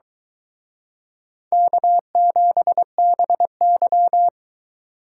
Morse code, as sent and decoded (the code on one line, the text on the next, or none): -.- --... -... -.--
K7BY